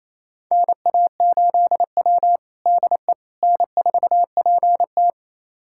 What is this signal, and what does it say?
Morse code, heard as a keyed tone, about 27 wpm, 700 Hz. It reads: NA8W DE N4PT
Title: USA Callsign DE Callsign - 4 Characters - 28wpm